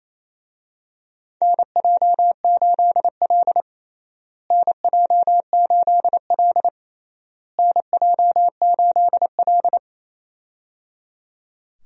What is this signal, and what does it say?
Morse code, heard as a keyed tone, about 28 wpm, 700 Hz. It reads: NJ8L NJ8L NJ8L
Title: USA Call Signs - 4 Characters - Repeated 3x - 28wpm